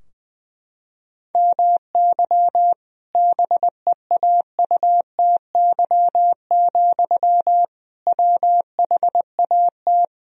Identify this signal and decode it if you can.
Morse code, heard as a keyed tone, about 20 wpm, 700 Hz. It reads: MY BEAUTY, WHAT
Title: Sentences from Top 1000 Words - Review Part 2 - 20wpm